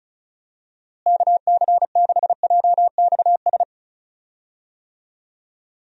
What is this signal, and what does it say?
Morse code, heard as a keyed tone, about 34 wpm, 700 Hz. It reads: KC6JXS
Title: USA Call Signs - 6 Characters - Repeated 2x - 35wpm